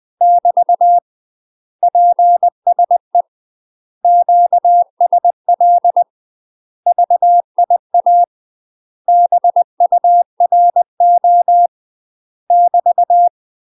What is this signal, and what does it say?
Morse code, heard as a keyed tone, about 20 wpm, 700 Hz. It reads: <BT> PSE QSL VIA BURO <BT>
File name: UK MASTER QSOs TRAINING / 20WPM / UK MASTER QSOs 0004